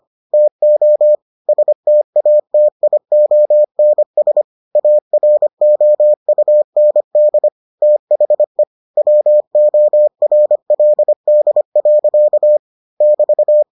Morse code, heard as a keyed tone, about 25 wpm, 600 Hz. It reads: TO STATIONS AROUND THE WORLD. <BT>